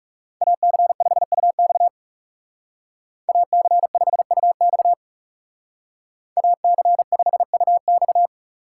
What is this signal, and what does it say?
Morse code, heard as a keyed tone, about 39 wpm, 700 Hz. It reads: AC5UX AC5UX AC5UX